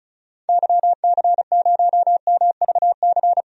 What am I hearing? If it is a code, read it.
Morse code, heard as a keyed tone, about 36 wpm, 700 Hz. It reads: YC0MVC
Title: DXCC-100 Call Signs - 6 Characters - 35wpm